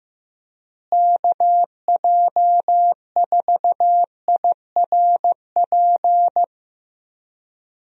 Morse code, heard as a keyed tone, about 15 wpm, 700 Hz. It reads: KJ4IRP
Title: USA Call Signs - 6 Characters - Repeated 3x - 15wpm